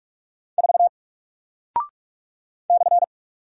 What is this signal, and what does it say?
Morse code, heard as a keyed tone, about 46 wpm, 700 Hz. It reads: 4 /
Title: Single Character (Letter:Number:Punctuation:Prosign) Alt 45wpm